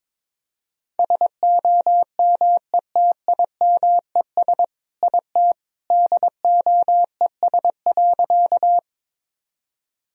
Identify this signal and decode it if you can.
Morse code, heard as a keyed tone, about 22 wpm, 700 Hz. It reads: SOMETIMES IT DOES.
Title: Sentences from Top 1000 Words - Review Part 3 - 22wpm